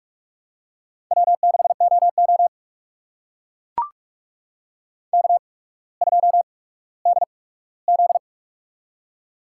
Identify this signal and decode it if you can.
Morse code, heard as a keyed tone, about 45 wpm, spaced as for 19 wpm, 700 Hz. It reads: W6OO K2DZ
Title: USA Call Signs - 4 Characters - 8x Character Spacing - 45wpm